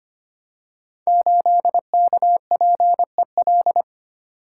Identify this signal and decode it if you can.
Morse code, heard as a keyed tone, about 25 wpm, 700 Hz. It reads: 8KPEL
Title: Code Groups - 5 Letter:Numbers - 25wpm